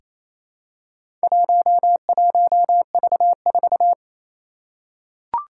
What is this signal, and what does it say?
Morse code, heard as a keyed tone, about 28 wpm, 700 Hz. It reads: 11V4
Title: Four Challenging Letter-Number Combinations 28wpm